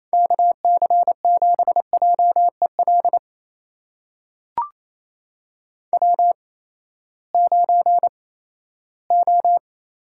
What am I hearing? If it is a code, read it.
Morse code, heard as a keyed tone, about 28 wpm, spaced as for 16 wpm, 700 Hz. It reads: KC7JEL W9O
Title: USA Call Signs - 8x Character Spacing - 28wpm